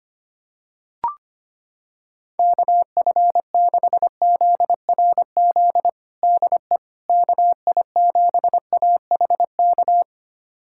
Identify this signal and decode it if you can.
Morse code, heard as a keyed tone, about 25 wpm, 700 Hz. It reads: KF6ZRZ DE KI7AHK
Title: USA Callsign DE Callsign 25wpm